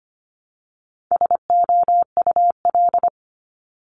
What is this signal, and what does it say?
Morse code, heard as a keyed tone, about 25 wpm, 700 Hz. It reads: SOUL
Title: Multi-letter Phonemes Repeated 3x 25wpm